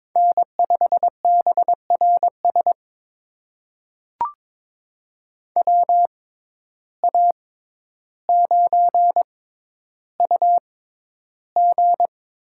Morse code, heard as a keyed tone, about 22 wpm, spaced as for 11 wpm, 700 Hz. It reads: N5BRS WA9UG
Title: USA Call Signs - 6x Character Spacing - 22wpm